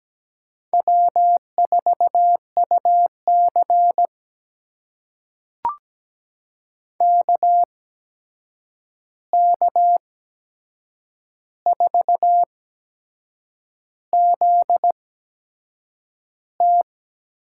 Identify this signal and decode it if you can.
Morse code, heard as a keyed tone, about 17 wpm, spaced as for 7 wpm, 700 Hz. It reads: W4UC KK4ZT
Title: USA Call Signs - 8x Character Spacing - 17wpm